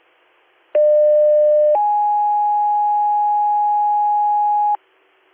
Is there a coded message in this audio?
Yes: two-tone page from 0.7 to 4.8 s, heard as a one-second tone, then a three-second tone, about 600 then 830 Hz, with noise.